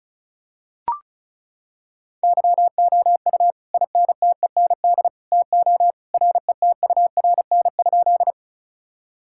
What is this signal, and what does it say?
Morse code, heard as a keyed tone, about 36 wpm, 700 Hz. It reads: YOU INTEND TO RETURN?
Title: Sentences from Top 1600 Words 35wpm